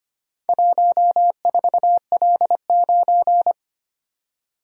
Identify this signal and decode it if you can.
Morse code, heard as a keyed tone, about 25 wpm, 700 Hz. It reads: 14L9
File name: practice / 25wpm / Four Letter-Number Combinations - 25wpm